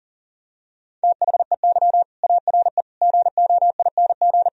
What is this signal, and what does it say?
Morse code, heard as a keyed tone, about 40 wpm, 700 Hz. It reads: THEY ARE GOING
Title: Sentences from Top 1000 Words - Review Part 3 - 40wpm